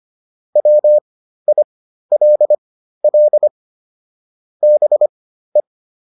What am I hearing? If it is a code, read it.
Morse code, heard as a keyed tone, about 25 wpm, spaced as for 13 wpm, 600 Hz. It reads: WILL BE